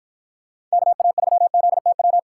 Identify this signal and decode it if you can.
Morse code, heard as a keyed tone, about 49 wpm, 700 Hz. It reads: KA3ZTW